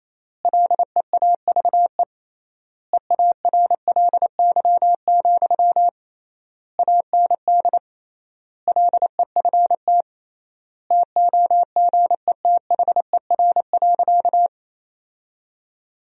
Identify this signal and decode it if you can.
Morse code, heard as a keyed tone, about 28 wpm, 700 Hz. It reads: LEAVE EARLY, AND LEFT TOGETHER.
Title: 3x Word-Spacing - Sentences from Top 400 Words - 28wpm